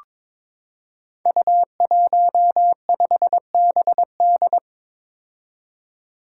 Morse code, heard as a keyed tone, about 22 wpm, 700 Hz. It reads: U15BD
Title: Five Challenging Letter-Number Combinations 22wpm